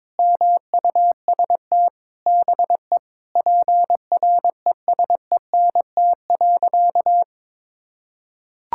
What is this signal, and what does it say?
Morse code, heard as a keyed tone, about 22 wpm, 700 Hz. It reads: MUST BE PRESENT.
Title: Sentences from Top 1600 Words 22wpm